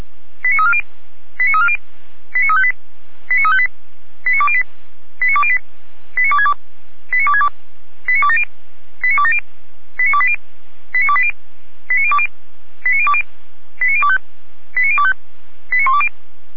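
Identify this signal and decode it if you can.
Five-tone selective call (ZVEI), heard as quick runs of five tones, tones of about 70 ms. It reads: 89340 89340 89358 89358 89208 89208 89262 89262 89270 89270 89280 89280 89020 89020 89025 89025 89120
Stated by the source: ZVEI five-tone call 89315 received by SDR on channel 455, recorded by feuerwehr-orlamuende.org fire brigade server.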